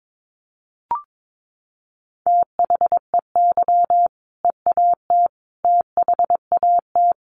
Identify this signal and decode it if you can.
Morse code, heard as a keyed tone, about 22 wpm, 700 Hz. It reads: THEY EAT THAT
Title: Sentences from Top 1400 Words 22wpm